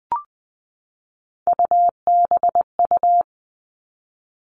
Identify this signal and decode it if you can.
Morse code, heard as a keyed tone, about 20 wpm, 700 Hz. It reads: UBU
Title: Three Challenging Letter-Number Combinations 20wpm